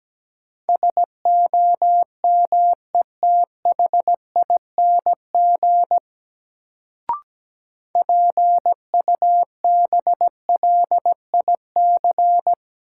Morse code, heard as a keyed tone, about 17 wpm, 700 Hz. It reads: SOMETHING PUBLIC
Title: Sentences from Top 600 Words - New Word Drill - 17wpm